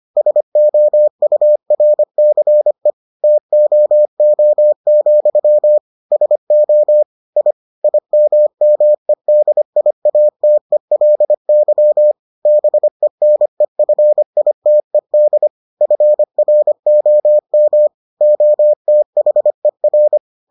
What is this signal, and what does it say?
Morse code, heard as a keyed tone, about 25 wpm, 600 Hz. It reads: SOURCE TOO, SO I IMMEDIATELY BENEFITED FROM OTHER